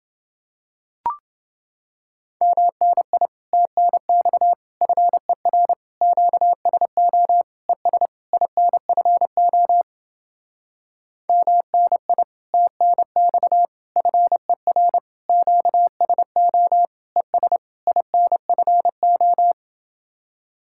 Morse code, heard as a keyed tone, about 29 wpm, 700 Hz. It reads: MNI TNX FER QSO ES INFO MNI TNX FER QSO ES INFO